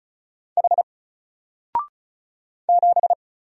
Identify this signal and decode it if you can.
Morse code, heard as a keyed tone, about 35 wpm, 700 Hz. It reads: H 7
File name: Single-Letter-Number-35wpm